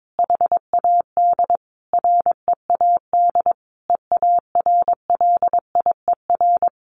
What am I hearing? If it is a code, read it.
Morse code, heard as a keyed tone, about 22 wpm, 700 Hz. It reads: HAD READ EARLIER
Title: A Sound Of Thunder 22wpm